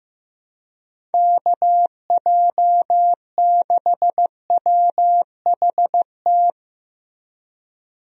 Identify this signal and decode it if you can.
Morse code, heard as a keyed tone, about 15 wpm, 700 Hz. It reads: KJ6WHT
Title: USA Call Signs - 6 Characters - Repeated 3x - 15wpm